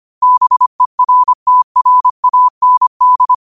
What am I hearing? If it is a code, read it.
Morse code, heard as a keyed tone, about 25 wpm, 1000 Hz. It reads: BERTRAND
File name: qotdfiles / 25wpm